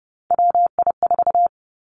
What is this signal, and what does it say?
Morse code, heard as a keyed tone, about 30 wpm, 700 Hz. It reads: WI4
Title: Three Challenging Letter-Number Combinations 30wpm